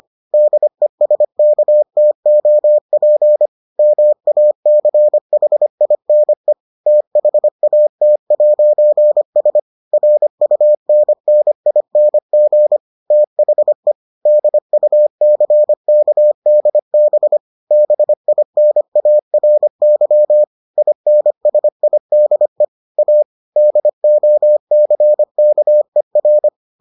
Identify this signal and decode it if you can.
Morse code, heard as a keyed tone, about 25 wpm, 600 Hz. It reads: DESKTOP MACHINE THAT'S RUNNING THE DUCKDB BINARY INSIDE A DOCKER